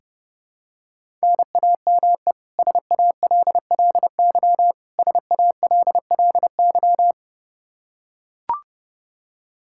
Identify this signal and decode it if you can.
Morse code, heard as a keyed tone, about 30 wpm, 700 Hz. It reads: NAME SALLY SALLY
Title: K1USN SST QSO Practice 30wpm